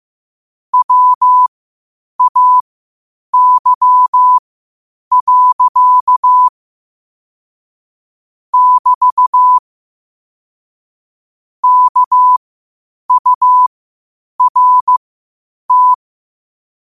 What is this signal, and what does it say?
Morse code, heard as a keyed tone, about 15 wpm, spaced as for 8 wpm, 1000 Hz. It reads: WAY. <BT> KURT
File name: qotdfiles / 8wpm